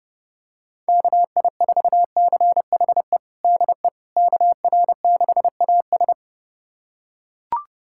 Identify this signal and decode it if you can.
Morse code, heard as a keyed tone, about 30 wpm, 690 Hz. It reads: KI4CHE DE KR6AS
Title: USA Callsign DE Callsign 30wpm